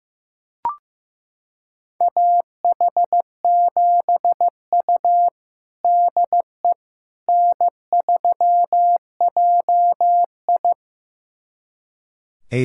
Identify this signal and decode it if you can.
Morse code, heard as a keyed tone, about 16 wpm, 700 Hz. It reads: AH7U DE N3JI
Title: USA Callsign DE Callsign - 4 Characters - 15wpm